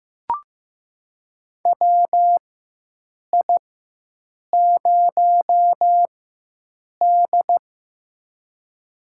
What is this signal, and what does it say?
Morse code, heard as a keyed tone, about 15 wpm, spaced as for 7 wpm, 700 Hz. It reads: WI0D
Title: USA Call Signs - 4 Characters - 4x Character Spacing - 15wpm